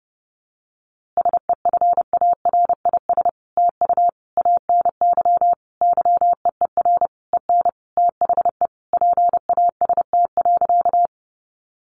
Morse code, heard as a keyed tone, about 30 wpm, 700 Hz. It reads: SEFARIS TU ANY YEER EN THE PAST.
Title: A Sound Of Thunder 30wpm